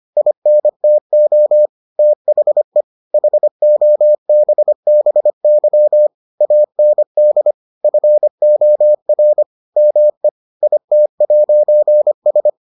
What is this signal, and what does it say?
Morse code, heard as a keyed tone, about 25 wpm, 600 Hz. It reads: INTO THE HOBBY AND FOR ME IT'S